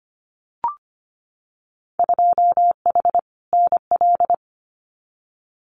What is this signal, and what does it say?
Morse code, heard as a keyed tone, about 25 wpm, 700 Hz. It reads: 2H NL
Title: Winter Field Day Exchanges Encore 25wpm